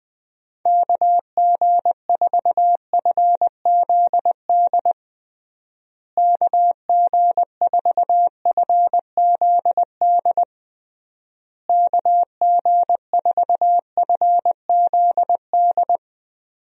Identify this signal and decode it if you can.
Morse code, heard as a keyed tone, about 20 wpm, 700 Hz. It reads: KG4FZD KG4FZD KG4FZD
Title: USA Call Signs - 6 Characters - Repeated 3x - 20wpm